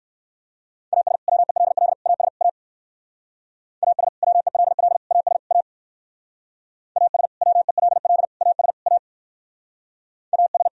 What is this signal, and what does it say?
Morse code, heard as a keyed tone, about 58 wpm, 700 Hz. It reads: AS WELL AS A AS WELL AS A AS WELL AS A AS